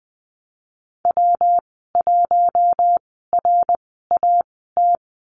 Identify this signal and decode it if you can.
Morse code, heard as a keyed tone, about 20 wpm, spaced as for 15 wpm, 700 Hz. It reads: W1RAT
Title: 154 Lesson 25g - Letter p - Call Signs 20wpm